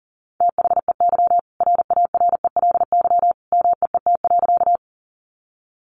Morse code, heard as a keyed tone, about 40 wpm, 700 Hz. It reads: THEY RARELY MEET.
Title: Sentences from Top 1400 Words 40wpm